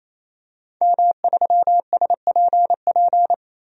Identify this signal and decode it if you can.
Morse code, heard as a keyed tone, about 27 wpm, 700 Hz. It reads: M3SPP
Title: DXCC-100 Call Signs - 5 Characters - 28wpm